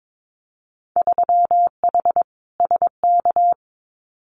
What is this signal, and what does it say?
Morse code, heard as a keyed tone, about 22 wpm, 700 Hz. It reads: 3H SK
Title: Winter Field Day Exchanges 22wpm